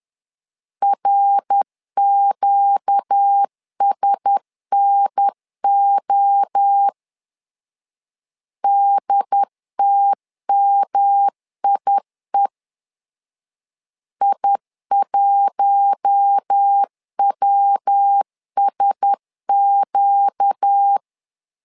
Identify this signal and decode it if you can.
Morse code, heard as a keyed tone, about 10 wpm, 800 Hz. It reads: RQSNO DTMIE I1WSQ
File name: LEZIONI / 4 Q1S 10wpm 2133